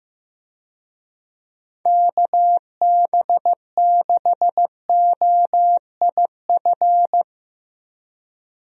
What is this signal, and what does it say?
Morse code, heard as a keyed tone, about 15 wpm, 700 Hz. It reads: KB6OIF
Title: USA Call Signs - 6 Characters - Repeated 2x - 15wpm